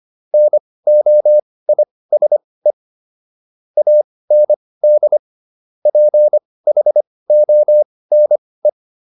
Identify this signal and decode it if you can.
Morse code, heard as a keyed tone, about 25 wpm, spaced as for 18 wpm, 600 Hz. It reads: NOISE AND PHONE